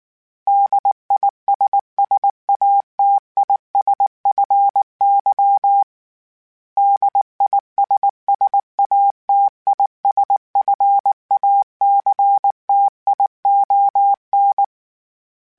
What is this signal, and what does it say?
Morse code, heard as a keyed tone, about 19 wpm, 800 Hz. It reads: DISSATISFY DISSATISFACTION